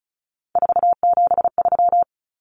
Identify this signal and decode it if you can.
Morse code, heard as a keyed tone, about 34 wpm, 700 Hz. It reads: 473
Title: Numbers - 3-Digit 35wpm